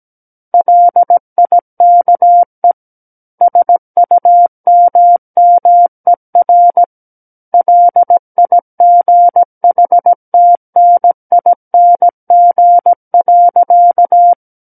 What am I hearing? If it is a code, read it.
Morse code, heard as a keyed tone, about 17 wpm, 700 Hz. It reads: LIKE SUMMER LIGHTNING.
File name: Book One / 08-Chapter 8 - 17 WPM